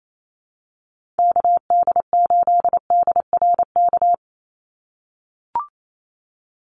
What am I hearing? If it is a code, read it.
Morse code, heard as a keyed tone, about 28 wpm, 700 Hz. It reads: KD8DRK
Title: USA Call Signs 28wpm